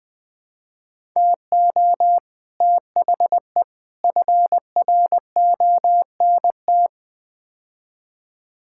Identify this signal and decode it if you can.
Morse code, heard as a keyed tone, about 21 wpm, 700 Hz. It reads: TO THE FRONT
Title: Build A Phrase - 5-Word Phrases - 20wpm